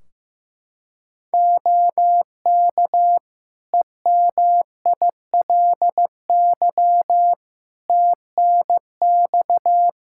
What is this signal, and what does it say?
Morse code, heard as a keyed tone, about 15 wpm, 700 Hz. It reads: OK EMILY TNX
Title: Practice QSO - Exchange 4 - Ending - 15wpm